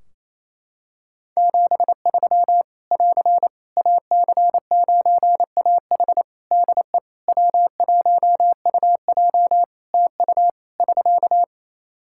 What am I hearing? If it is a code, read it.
Morse code, heard as a keyed tone, about 27 wpm, 700 Hz. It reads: 73 <AR> AC9AH DE W1UJ TU <SK>
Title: K1USN SST QSO Practice 28wpm